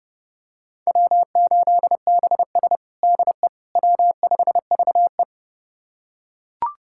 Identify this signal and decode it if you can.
Morse code, heard as a keyed tone, about 32 wpm, 700 Hz. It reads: W8BS DE W5VE
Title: USA Callsign DE Callsign - 4 Characters - 30wpm